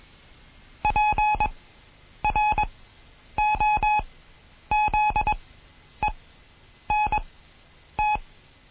Morse code, heard as a keyed tone, about 21 wpm, spaced as for 9 wpm, 850 Hz. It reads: PROZENT
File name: Morsen / Kurs A 20